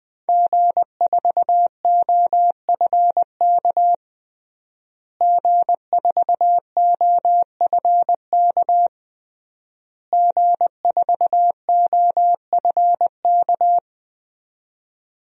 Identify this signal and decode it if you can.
Morse code, heard as a keyed tone, about 21 wpm, 700 Hz. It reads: G4OFK G4OFK G4OFK
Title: DXCC-100 Call Signs - 5 Characters - Repeated 3x - 20wpm